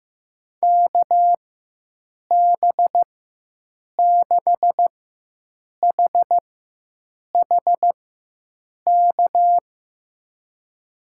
Morse code, heard as a keyed tone, about 16 wpm, spaced as for 7 wpm, 700 Hz. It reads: KB6HHK
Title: USA Call Signs - 6 Characters - 4x Character Spacing - 15wpm